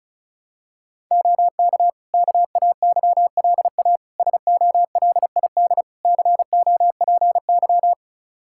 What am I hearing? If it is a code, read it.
Morse code, heard as a keyed tone, about 34 wpm, 700 Hz. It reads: OK KAYLA SOLID COPY
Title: K1USN SST QSO Practice 35wpm